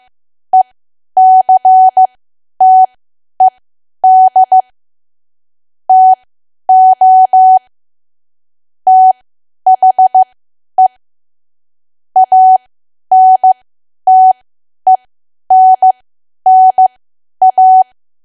Morse code, heard as a keyed tone, about 15 wpm, spaced as for 10 wpm, 750 Hz. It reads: ECTED TO THE ANTENNA
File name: Archive / 10 WPM / 10WPM